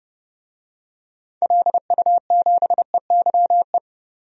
Morse code, heard as a keyed tone, about 30 wpm, 700 Hz. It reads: LU7EYE
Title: DXCC-100 Call Signs - 6 Characters - 30wpm